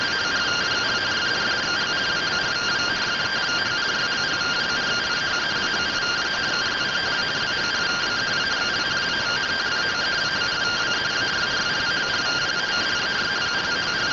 Two-tone teleprinter signal, about 170 Hz shift.